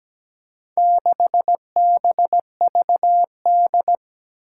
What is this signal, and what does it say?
Morse code, heard as a keyed tone, about 17 wpm, 700 Hz. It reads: 6BVD
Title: Four Challenging Letter-Number Combinations 17wpm